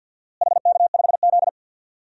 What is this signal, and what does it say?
Morse code, heard as a keyed tone, about 52 wpm, 700 Hz. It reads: HK5Z
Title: DXCC-100 Call Signs - 4 Characters - Repeated 2x - 50wpm